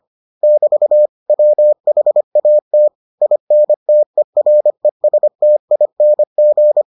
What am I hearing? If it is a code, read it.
Morse code, heard as a keyed tone, about 25 wpm, 600 Hz. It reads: <BT> WHAT INTERESTING